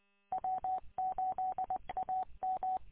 Morse code, heard as a keyed tone, about 20 wpm, 750 Hz. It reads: W8AM
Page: W8UM 145.230 Ann Arbor